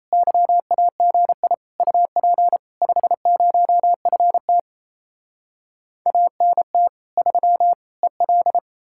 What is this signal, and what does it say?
Morse code, heard as a keyed tone, about 37 wpm, 700 Hz. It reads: YAGI UP 50FT ANT 3 EL